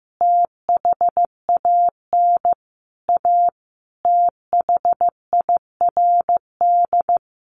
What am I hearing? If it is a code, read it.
Morse code, heard as a keyed tone, about 15 wpm, 700 Hz. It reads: THAN A THIRD
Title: Sets of 5 Words Encore2 15wpm